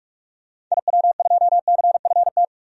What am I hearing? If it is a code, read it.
Morse code, heard as a keyed tone, about 46 wpm, 700 Hz. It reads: IW2CFT
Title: DXCC-100 Call Signs - 6 Characters - 45wpm